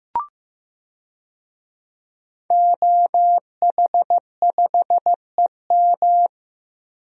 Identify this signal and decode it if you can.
Morse code, heard as a keyed tone, about 15 wpm, 700 Hz. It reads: OH5EM
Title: DXCC-100 Call Signs - 5 Characters - Repeated 3x - 15wpm